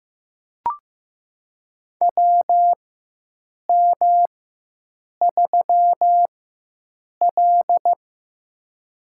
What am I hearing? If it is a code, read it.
Morse code, heard as a keyed tone, about 15 wpm, spaced as for 7 wpm, 700 Hz. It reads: WM3L
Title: USA Call Signs - 4 Characters - 4x Character Spacing - 15wpm